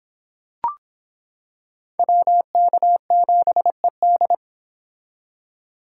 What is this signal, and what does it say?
Morse code, heard as a keyed tone, about 25 wpm, 700 Hz. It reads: WK7ED